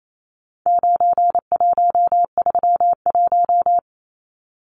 Morse code, heard as a keyed tone, about 28 wpm, 700 Hz. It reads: 9131
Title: Numbers - 4-Digit 28wpm